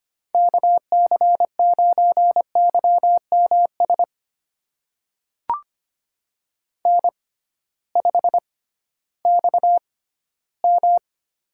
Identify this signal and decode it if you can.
Morse code, heard as a keyed tone, about 25 wpm, spaced as for 14 wpm, 700 Hz. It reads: KC9YMS N5XM
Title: USA Call Signs - 6x Character Spacing - 25wpm